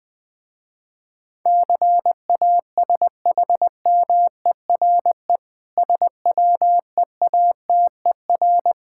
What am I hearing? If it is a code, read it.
Morse code, heard as a keyed tone, about 20 wpm, 700 Hz. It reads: CASHMERE SWEATER
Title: Difficult Words with Hints - 20wpm